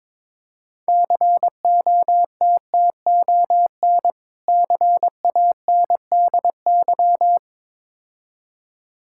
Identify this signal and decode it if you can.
Morse code, heard as a keyed tone, about 22 wpm, 690 Hz. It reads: COTTON CANDY
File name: Difficult Words with Hints - 22wpm